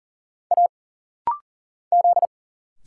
Morse code, heard as a keyed tone, about 40 wpm, 700 Hz. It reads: A Z
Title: Single Letters - Rapid-Fire 40wpm